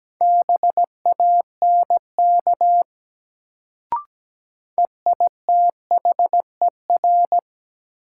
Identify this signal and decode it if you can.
Morse code, heard as a keyed tone, about 17 wpm, 700 Hz. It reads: BANK EITHER
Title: Sets of 5 Words - New Word Drill - 17wpm